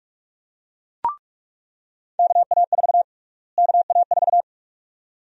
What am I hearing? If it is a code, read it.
Morse code, heard as a keyed tone, about 46 wpm, 700 Hz. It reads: KA4 KA4
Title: USA Call Sign Prefixes - Repeated 2x - 45wpm